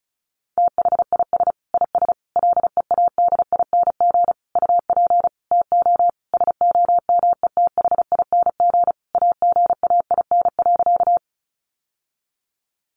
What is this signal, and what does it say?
Morse code, heard as a keyed tone, about 34 wpm, 700 Hz. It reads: THIS IS LEADING UP TO SOMETHING AGAIN.
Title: Sentences from Top 1100 Words 35wpm